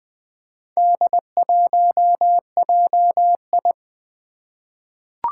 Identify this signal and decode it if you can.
Morse code, heard as a keyed tone, about 20 wpm, 700 Hz. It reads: D1JI
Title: Four Challenging Letter-Number Combinations 20wpm